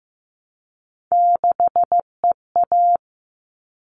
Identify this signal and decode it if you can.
Morse code, heard as a keyed tone, about 15 wpm, 700 Hz. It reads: 6EA
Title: All Cut Numbers - 3-Digit 15wpm